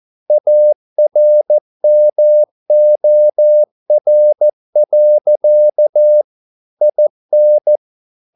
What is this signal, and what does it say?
Morse code, heard as a keyed tone, about 14 wpm, 600 Hz. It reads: ARMOR. IN